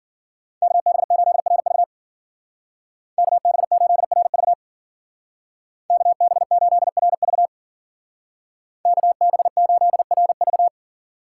Morse code, heard as a keyed tone, about 52 wpm, 700 Hz. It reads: KB8RV KB8RV KB8RV KB8RV